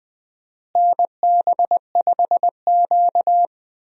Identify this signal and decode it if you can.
Morse code, heard as a keyed tone, about 20 wpm, 700 Hz. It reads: NB5Q
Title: USA Call Signs - 4 Characters - Repeated 3x - 20wpm